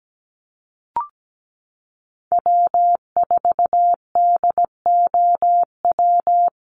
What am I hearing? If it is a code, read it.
Morse code, heard as a keyed tone, about 17 wpm, 700 Hz. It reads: W4DOW
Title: NA SOTA Chaser Callsigns 17wpm